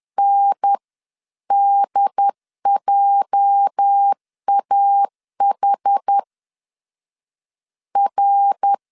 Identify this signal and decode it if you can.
Morse code, heard as a keyed tone, about 10 wpm, 800 Hz. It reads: NDJAH R